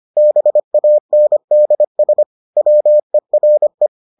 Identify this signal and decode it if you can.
Morse code, heard as a keyed tone, about 25 wpm, 600 Hz. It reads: BANDS WERE